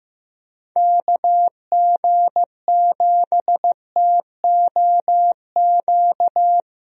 Morse code, heard as a keyed tone, about 15 wpm, 700 Hz. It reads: KG7TOQ
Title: USA Call Signs - 6 Characters - Repeated 3x - 15wpm